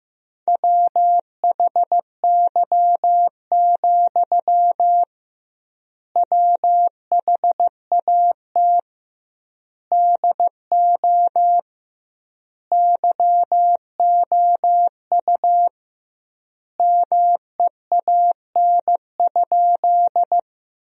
Morse code, heard as a keyed tone, about 15 wpm, 700 Hz. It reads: WHY, WHAT DO YOU MEAN?
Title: 2x Word-Spacing - Sentences from Top 200 Words - 15wpm